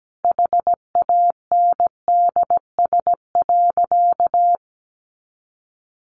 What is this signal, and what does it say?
Morse code, heard as a keyed tone, about 17 wpm, 700 Hz. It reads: HANDS.
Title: Easy Phrases 17wpm